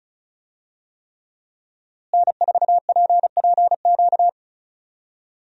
Morse code, heard as a keyed tone, about 36 wpm, 700 Hz. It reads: N4PPQ
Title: USA Call Signs - 5 Characters - Repeated 3x - 35wpm